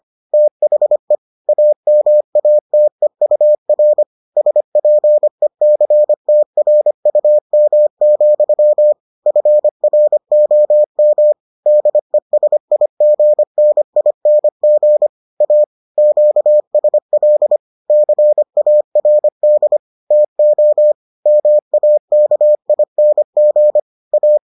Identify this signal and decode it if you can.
Morse code, heard as a keyed tone, about 25 wpm, 600 Hz. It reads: THE AMATEUR SPECTRUM, FROM DESIGNING A QSL CARD TO MAKING A